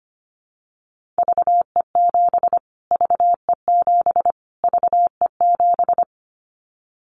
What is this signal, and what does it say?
Morse code, heard as a keyed tone, about 25 wpm, 700 Hz. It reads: VE7 VE7 VE7
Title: Callsign Prefixes for Top 25 DXCC - Repeated 3x - 25wpm